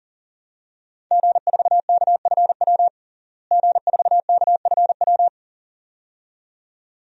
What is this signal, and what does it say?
Morse code, heard as a keyed tone, about 40 wpm, 700 Hz. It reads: G4KFW G4KFW
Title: DXCC-100 Call Signs - 5 Characters - Repeated 2x - 40wpm